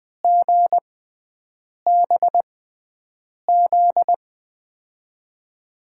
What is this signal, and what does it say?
Morse code, heard as a keyed tone, about 20 wpm, spaced as for 7 wpm, 700 Hz. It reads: GBZ
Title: USA Call Signs - 6x Character Spacing - 20wpm